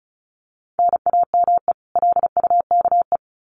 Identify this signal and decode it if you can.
Morse code, heard as a keyed tone, about 34 wpm, 700 Hz. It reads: NAME LUKE
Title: Common QSO Phrases 35wpm